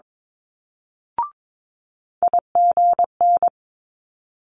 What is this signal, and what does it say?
Morse code, heard as a keyed tone, about 22 wpm, 700 Hz. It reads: IGN
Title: Multi-letter Phonemes Quick Fire 22wpm